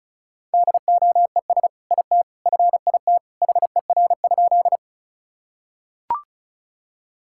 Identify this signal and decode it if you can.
Morse code, heard as a keyed tone, about 34 wpm, 700 Hz. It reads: DOES IT FIT HER?
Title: Sentences from Top 1000 Words - Review Part 5 - 35wpm